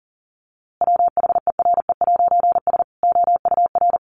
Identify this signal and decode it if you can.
Morse code, heard as a keyed tone, about 40 wpm, 700 Hz. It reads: WHERE'S OUR
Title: A Sound Of Thunder 40wpm